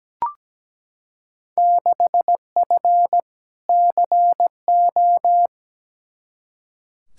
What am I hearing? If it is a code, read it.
Morse code, heard as a keyed tone, about 17 wpm, 700 Hz. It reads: 6F CO
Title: Field Day Exchanges Encore - 17wpm